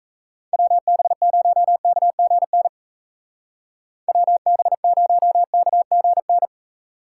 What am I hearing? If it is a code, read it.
Morse code, heard as a keyed tone, about 34 wpm, 700 Hz. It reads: WB0KGN WB0KGN
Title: Speed Racing - USA Call Signs 28wpm